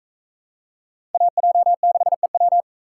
Morse code, heard as a keyed tone, about 44 wpm, 700 Hz. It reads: AJ6EW